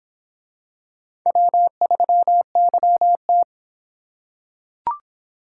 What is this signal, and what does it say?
Morse code, heard as a keyed tone, about 26 wpm, 700 Hz. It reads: W3YT